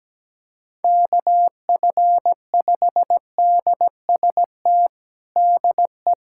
Morse code, heard as a keyed tone, about 17 wpm, 700 Hz. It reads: KF5DST DE
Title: USA Callsign DE Callsign 17wpm